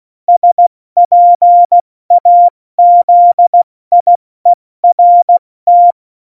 Morse code, heard as a keyed tone, about 16 wpm, 700 Hz. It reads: SPAZIERT
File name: Morsekurs / Witze / Witz 15. Mai 2018, 70 F80